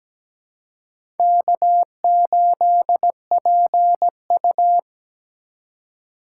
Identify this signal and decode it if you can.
Morse code, heard as a keyed tone, about 17 wpm, 700 Hz. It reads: K8PU
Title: USA Call Signs - 4 Characters - 3x Character Spacing - 17wpm